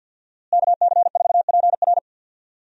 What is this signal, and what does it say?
Morse code, heard as a keyed tone, about 51 wpm, 700 Hz. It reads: KC4PR
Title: USA Call Signs - 5 Characters - Repeated 2x - 50wpm